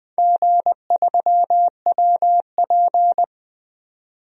Morse code, heard as a keyed tone, about 20 wpm, 700 Hz. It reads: G3WP
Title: DXCC-100 Call Signs - 4 Characters - 20wpm